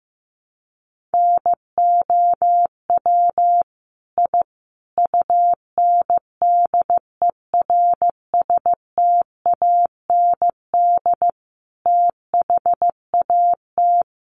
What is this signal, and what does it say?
Morse code, heard as a keyed tone, about 14 wpm, 700 Hz. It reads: NOW I UNDERSTAND THAT
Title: Sentences from Top 1000 Words 15wpm